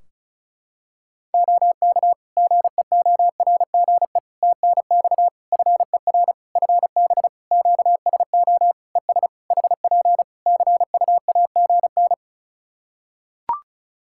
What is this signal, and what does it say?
Morse code, heard as a keyed tone, about 34 wpm, 700 Hz. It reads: OK GEORGE TNX FER FB QSO ES HP CUAGN
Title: K1USN SST QSO Practice 35wpm